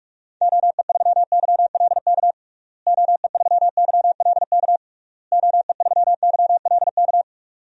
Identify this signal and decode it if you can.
Morse code, heard as a keyed tone, about 46 wpm, 700 Hz. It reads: OE3YLK OE3YLK OE3YLK
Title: DXCC-100 Call Signs - 6 Characters - Repeated 3x - 45wpm